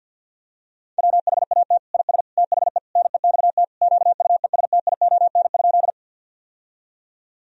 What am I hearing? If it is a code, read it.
Morse code, heard as a keyed tone, about 51 wpm, 700 Hz. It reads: WHAT IS THE NEXT QUESTION?
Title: Sentences from Top 1000 Words - Review Part 1 - 50wpm